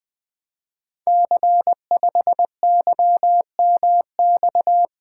Morse code, heard as a keyed tone, about 20 wpm, 700 Hz. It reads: C5YMX
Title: Code Groups - 5 Letter:Numbers - 20wpm